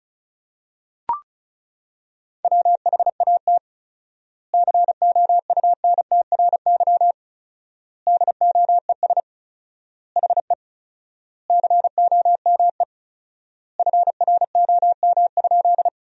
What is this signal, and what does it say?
Morse code, heard as a keyed tone, about 34 wpm, 700 Hz. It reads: WHAT COUNTRY DOES HE COME FROM?
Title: 4x Word-Spacing - Sentences from Top 300 Words - 35wpm